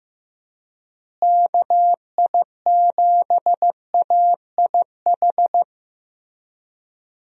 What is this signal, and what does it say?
Morse code, heard as a keyed tone, about 15 wpm, 700 Hz. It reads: KI7AIH
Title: USA Call Signs - 6 Characters - Repeated 3x - 15wpm